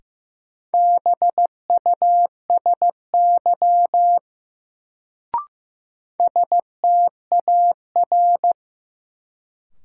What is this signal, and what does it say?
Morse code, heard as a keyed tone, about 15 wpm, 700 Hz. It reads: BUSY STAR
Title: Sentences from Top 1000 Words - Review Part 2 - New Word Drill - 15wpm